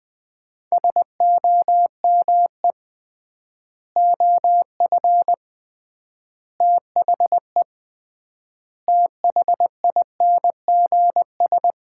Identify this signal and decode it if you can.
Morse code, heard as a keyed tone, about 20 wpm, 700 Hz. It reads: SOME OF THE THINGS
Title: 3x Word-Spacing - Sets of 5 Words - 20wpm